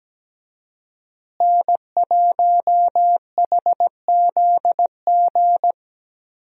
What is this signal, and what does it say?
Morse code, heard as a keyed tone, about 17 wpm, 700 Hz. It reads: N1HZG
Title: USA Call Signs - 5 Characters - Repeated 3x - 17wpm